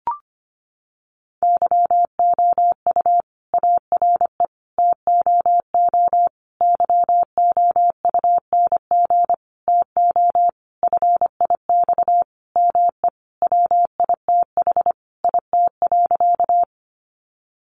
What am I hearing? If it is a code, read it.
Morse code, heard as a keyed tone, about 25 wpm, 700 Hz. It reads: YOU ARE TOO YOUNG TO FIX ME WITH IT.
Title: Sentences from Top 1400 Words 25wpm